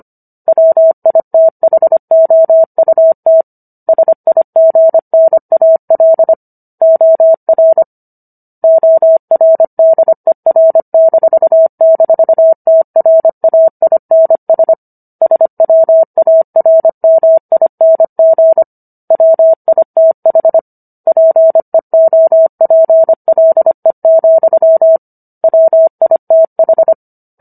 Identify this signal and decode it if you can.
Morse code, heard as a keyed tone, about 25 wpm, 650 Hz. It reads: WITHOUT SIGNAL OR ORDER--TRAINS SWARMING WITH PEOPLE, WITH